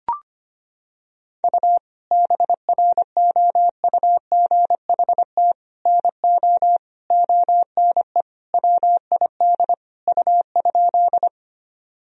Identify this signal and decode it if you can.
Morse code, heard as a keyed tone, about 25 wpm, 700 Hz. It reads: U BROUGHT NO ONE WID U?
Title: Sentences from Top 500 Words and Abbreviations 25wpm